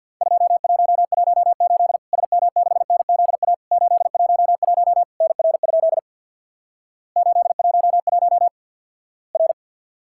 Morse code, heard as a keyed tone, about 50 wpm, 700 Hz. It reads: 2118 SM6NZA 811 NR? 811 R
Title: Worked All Europe DX Contest - QTC Exchange with Repeats - 50wpm